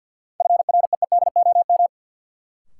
Morse code, heard as a keyed tone, about 52 wpm, 700 Hz. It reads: FREEDOM
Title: Sentences from Top 800 Words - New Word Drill - 50wpm